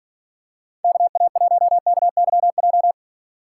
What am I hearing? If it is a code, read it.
Morse code, heard as a keyed tone, about 40 wpm, 700 Hz. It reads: KA1KYJ
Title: Speed Racing - USA Call Signs 35wpm